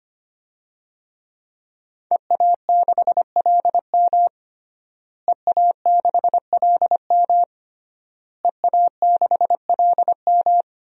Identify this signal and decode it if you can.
Morse code, heard as a keyed tone, about 25 wpm, 700 Hz. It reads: EA6LM EA6LM EA6LM
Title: DXCC-100 Call Signs - 5 Characters - Repeated 3x - 25wpm